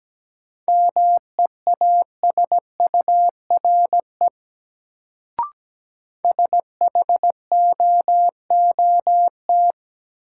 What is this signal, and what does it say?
Morse code, heard as a keyed tone, about 17 wpm, 700 Hz. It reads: MEASURE SHOOT
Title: Sentences from Top 1200 Words - New Word Drill Alt - 17wpm